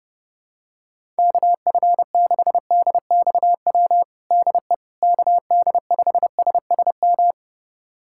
Morse code, heard as a keyed tone, about 32 wpm, 700 Hz. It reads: KF6DXW DE KD5SSM
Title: USA Callsign DE Callsign - 6 Characters - 30wpm